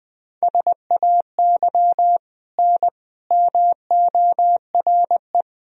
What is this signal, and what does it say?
Morse code, heard as a keyed tone, about 21 wpm, 700 Hz. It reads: SAY N MORE
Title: Sentences from Top 200 Words and Abbreviations 20wpm